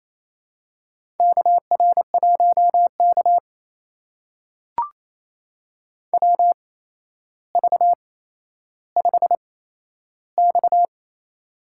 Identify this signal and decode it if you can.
Morse code, heard as a keyed tone, about 28 wpm, spaced as for 12 wpm, 700 Hz. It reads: KR1K WV5X
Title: USA Call Signs - 4 Characters - 8x Character Spacing - 28wpm